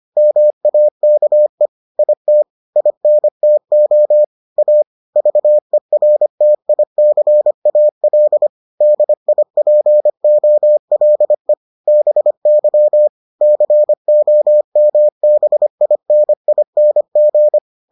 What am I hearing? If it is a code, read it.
Morse code, heard as a keyed tone, about 25 wpm, 600 Hz. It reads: MAKE IT INTO A VERTICAL DIPOLE BY COMBINING